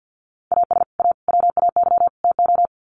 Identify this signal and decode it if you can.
Morse code, heard as a keyed tone, about 50 wpm, 700 Hz. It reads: AS A WAY TO
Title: Sets of 4 Words 50wpm